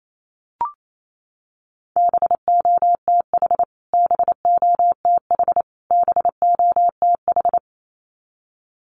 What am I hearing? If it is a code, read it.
Morse code, heard as a keyed tone, about 28 wpm, 700 Hz. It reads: BOTH BOTH BOTH
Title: Top 500 Words Repeated 28wpm